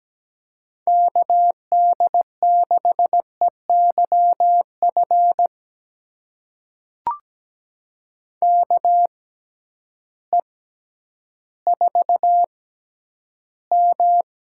Morse code, heard as a keyed tone, about 17 wpm, spaced as for 10 wpm, 700 Hz. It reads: KD6EYF KE4M
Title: USA Call Signs - 6 Characters - 6x Character Spacing - 17wpm